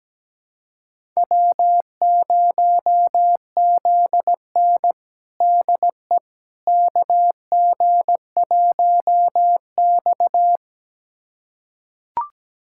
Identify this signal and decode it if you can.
Morse code, heard as a keyed tone, about 17 wpm, 700 Hz. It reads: W0ZN DE KG1X
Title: USA Callsign DE Callsign - 4 Characters - 17wpm